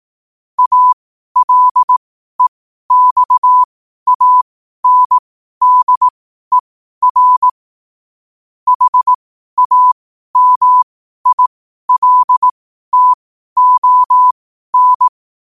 Morse code, heard as a keyed tone, about 18 wpm, spaced as for 12 wpm, 1000 Hz. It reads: ALEXANDER HAMILTON
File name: qotdfiles / 12wpm